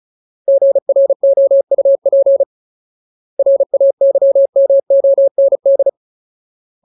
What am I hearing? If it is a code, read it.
Morse code, heard as a keyed tone, about 36 wpm, 550 Hz. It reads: GROUP RAYMOND
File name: Ultimate Word List / 35WPM Ultimate